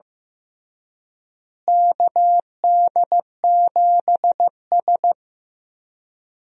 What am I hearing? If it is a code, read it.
Morse code, heard as a keyed tone, about 15 wpm, 700 Hz. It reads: KD7S
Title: USA Call Signs - 4 Characters - Repeated 3x - 15wpm